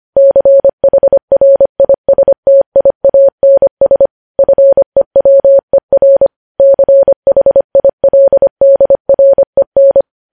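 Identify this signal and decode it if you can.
Morse code, heard as a keyed tone, about 25 wpm, 550 Hz. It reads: CHRISTIANS FEWER CHILDREN
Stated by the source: News headlines sent at 25.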